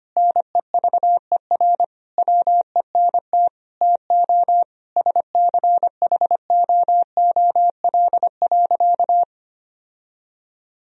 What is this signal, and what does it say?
Morse code, heard as a keyed tone, about 25 wpm, 700 Hz. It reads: NEVER WENT TO SCHOOL.
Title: Sentences from Top 1000 Words - Review Part 1 - 25wpm